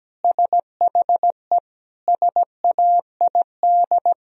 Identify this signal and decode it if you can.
Morse code, heard as a keyed tone, about 17 wpm, 700 Hz. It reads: SHE SAID
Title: Sentences from Top 1000 Words - Review Part 4 - 17wpm